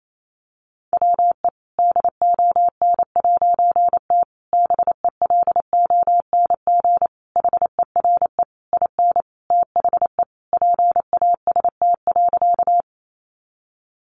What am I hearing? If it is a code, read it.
Morse code, heard as a keyed tone, about 28 wpm, 700 Hz. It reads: WE DON'T BELONG HERE IN THE PAST.
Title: A Sound Of Thunder 28wpm